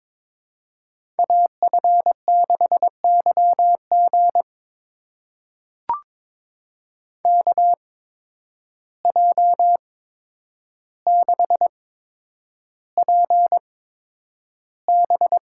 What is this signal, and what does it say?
Morse code, heard as a keyed tone, about 22 wpm, spaced as for 9 wpm, 700 Hz. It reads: AF6YG KJ6PB
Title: USA Call Signs - 8x Character Spacing - 22wpm